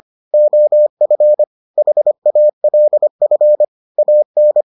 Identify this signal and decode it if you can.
Morse code, heard as a keyed tone, about 25 wpm, 600 Hz. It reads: OF HALF AN